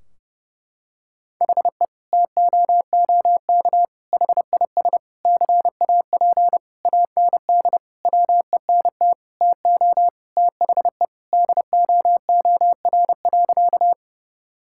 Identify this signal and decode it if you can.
Morse code, heard as a keyed tone, about 30 wpm, 700 Hz. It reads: HE TOOK HIS CAP AND WENT TO THE DOOR.
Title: Sentences from Top 1600 Words 30wpm